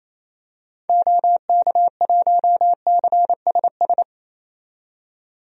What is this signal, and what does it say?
Morse code, heard as a keyed tone, about 28 wpm, 700 Hz. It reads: OK1CSS
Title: DXCC-100 Call Signs - 6 Characters - 28wpm